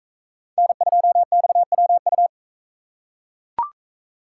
Morse code, heard as a keyed tone, about 42 wpm, 700 Hz. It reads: N2XWU